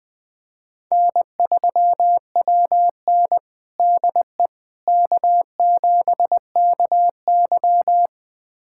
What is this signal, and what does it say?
Morse code, heard as a keyed tone, about 21 wpm, 700 Hz. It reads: N3WN DE K7KY
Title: USA Callsign DE Callsign - 4 Characters - 20wpm